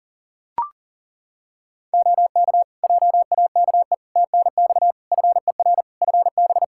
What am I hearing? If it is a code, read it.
Morse code, heard as a keyed tone, about 40 wpm, 700 Hz. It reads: OK JAKE TNX FER FB
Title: Practice QSO - Exchange 4 - Ending - 40wpm